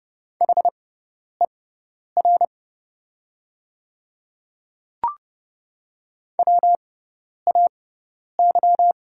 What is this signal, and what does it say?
Morse code, heard as a keyed tone, about 30 wpm, spaced as for 10 wpm, 700 Hz. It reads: HER WAY
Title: 6x Character Spacing - Top 300 Words - 30wpm